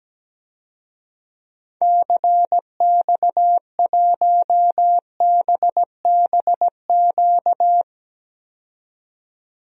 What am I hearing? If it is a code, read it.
Morse code, heard as a keyed tone, about 17 wpm, 700 Hz. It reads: CX1BBQ
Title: DXCC-100 Call Signs - 6 Characters - 17wpm